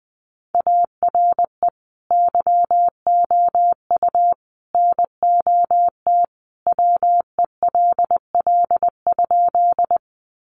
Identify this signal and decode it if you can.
Morse code, heard as a keyed tone, about 20 wpm, 700 Hz. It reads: ARE YOU NOT WELL?
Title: Sentences from Top 600 Words 20wpm